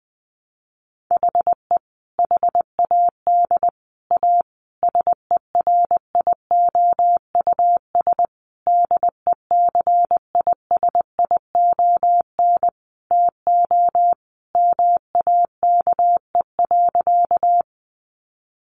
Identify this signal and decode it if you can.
Morse code, heard as a keyed tone, about 20 wpm, 700 Hz. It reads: HE HAD A SERIOUS DECISION TO MAKE.
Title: Sentences from Top 1100 Words 20wpm